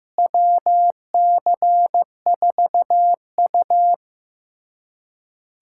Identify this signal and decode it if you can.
Morse code, heard as a keyed tone, about 15 wpm, 700 Hz. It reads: WC4U
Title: USA Callsign DE Callsign - 4 Characters - 15wpm